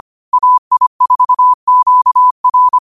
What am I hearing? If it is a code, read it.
Morse code, heard as a keyed tone, about 25 wpm, 1000 Hz. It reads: AI4QR
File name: qotdfiles / 25wpm